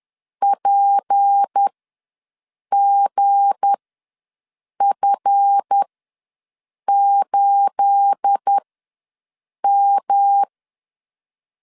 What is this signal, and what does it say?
Morse code, heard as a keyed tone, about 10 wpm, spaced as for 6 wpm, 800 Hz. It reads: PGF8M
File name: LEZIONI / 9 28 ws 10wpm 4124